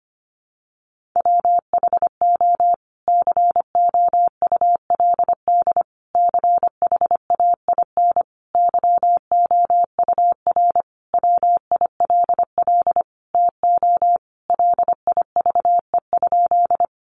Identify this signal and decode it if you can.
Morse code, heard as a keyed tone, about 25 wpm, 700 Hz. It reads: WHO COULD CHAIN YOUR WILL TO LIVE?
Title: Sentences from Top 1500 Words 25wpm